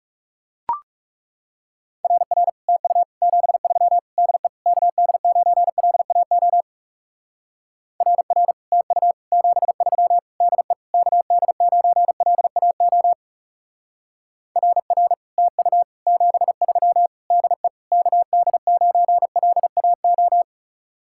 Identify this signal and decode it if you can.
Morse code, heard as a keyed tone, about 37 wpm, 700 Hz. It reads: RR TU 73 DE KD9LAO RR TU 73 DE KD9LAO RR TU 73 DE KD9LAO
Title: Speed-Racing Common QSO Phrases - 30wpm